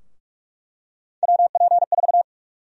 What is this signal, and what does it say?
Morse code, heard as a keyed tone, about 44 wpm, 700 Hz. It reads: WP4
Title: USA Call Sign Prefixes - 45wpm